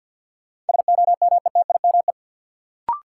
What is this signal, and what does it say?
Morse code, heard as a keyed tone, about 51 wpm, 700 Hz. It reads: SOMETIME
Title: Sentences from Top 1000 Words - Review Part 6 - New Word Drill - 50wpm